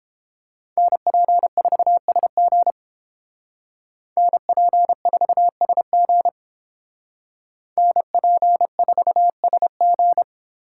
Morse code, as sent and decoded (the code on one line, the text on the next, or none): -. .--. ....- ... --. / -. .--. ....- ... --. / -. .--. ....- ... --.
NP4SG NP4SG NP4SG